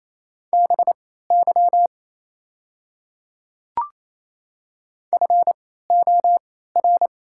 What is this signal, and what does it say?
Morse code, heard as a keyed tone, about 28 wpm, spaced as for 16 wpm, 690 Hz. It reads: BY FOR
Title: 3x Character Spacing - Top 100 Words - 28wpm